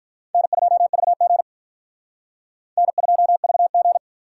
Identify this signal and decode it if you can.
Morse code, heard as a keyed tone, about 43 wpm, 700 Hz. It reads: N2VG N2VG
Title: Speed Racing - USA 1x2 Call Signs 35wpm